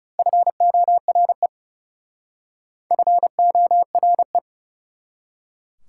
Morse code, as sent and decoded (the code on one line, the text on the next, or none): ..-. --- .-. / ..-. --- .-. .
FORE FORE